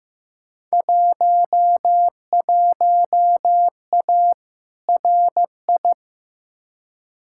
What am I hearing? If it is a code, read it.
Morse code, heard as a keyed tone, about 15 wpm, 700 Hz. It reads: 11A RI
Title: Field Day Exchanges Encore - 15wpm